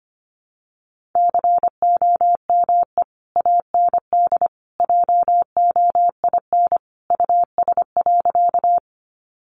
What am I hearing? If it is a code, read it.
Morse code, heard as a keyed tone, about 25 wpm, 700 Hz. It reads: COME AND JOIN US.
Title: Sentences from Top 1400 Words 25wpm